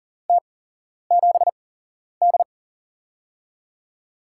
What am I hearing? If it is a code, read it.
Morse code, heard as a keyed tone, about 40 wpm, spaced as for 11 wpm, 700 Hz. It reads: T7D
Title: USA Call Signs - 8x Character Spacing - 40wpm